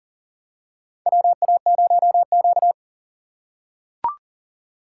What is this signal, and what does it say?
Morse code, heard as a keyed tone, about 40 wpm, 700 Hz. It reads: WA0Q
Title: USA Call Signs - 4 Characters - 6x Character Spacing - 40wpm